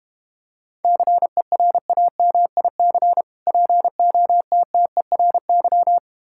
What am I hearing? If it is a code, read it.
Morse code, heard as a keyed tone, about 32 wpm, 700 Hz. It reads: CERAMIC POTTERY